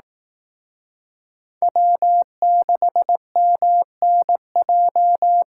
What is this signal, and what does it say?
Morse code, heard as a keyed tone, about 18 wpm, 700 Hz. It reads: W6MNJ
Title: Speed Racing - USA 1x3 Call Signs 15wpm